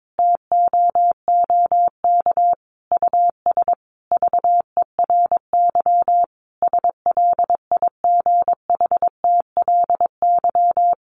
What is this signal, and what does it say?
Morse code, heard as a keyed tone, about 22 wpm, 700 Hz. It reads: TOOK US VERY SLIGHTLY
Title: Sentences from Top 1300 Words 22wpm